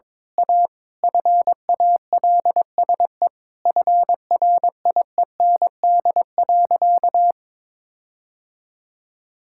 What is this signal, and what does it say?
Morse code, heard as a keyed tone, about 22 wpm, 700 Hz. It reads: A FALSE FRIEND.
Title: Sentences from Top 1600 Words 22wpm